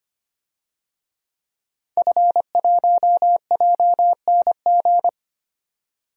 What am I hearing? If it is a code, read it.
Morse code, heard as a keyed tone, about 25 wpm, 700 Hz. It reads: F1JNG
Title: DXCC-100 Call Signs - 5 Characters - 25wpm